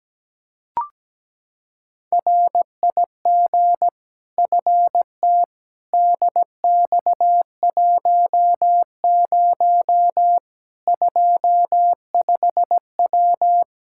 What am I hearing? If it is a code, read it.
Morse code, heard as a keyed tone, about 17 wpm, 700 Hz. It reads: RIG FT DX10 25W
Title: K1USN SST QSO Practice 17wpm